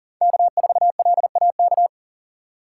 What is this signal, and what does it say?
Morse code, heard as a keyed tone, about 39 wpm, 700 Hz. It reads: K4LAK
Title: USA Call Signs - 5 Characters - 4x Character Spacing - 40wpm